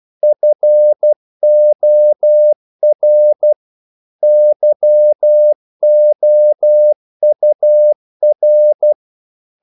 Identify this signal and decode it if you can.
Morse code, heard as a keyed tone, about 12 wpm, 600 Hz. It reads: FOR YOUR
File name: CW / Practice / EnglishQuotations0312